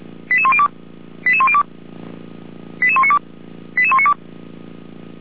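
Five-tone selective call (ZVEI), heard as quick runs of five tones, tones of about 70 ms. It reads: 88102 88102 88192 88192
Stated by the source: Five-tone ZVEI selective call 89240, received by SDR, from Feuerwehr Orlamünde.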